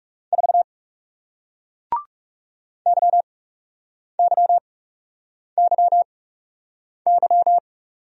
Morse code, heard as a keyed tone, about 40 wpm, spaced as for 8 wpm, 700 Hz. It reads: 4 YYYY